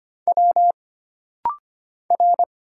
Morse code, heard as a keyed tone, about 25 wpm, 700 Hz. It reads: W R
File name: Single Character (Letter:Number:Punctuation:Prosign) - Rapid-Fire - 25wpm